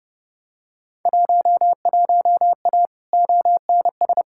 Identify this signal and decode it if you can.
Morse code, heard as a keyed tone, about 30 wpm, 700 Hz. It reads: 11A ONS
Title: Field Day Exchanges Encore - 30wpm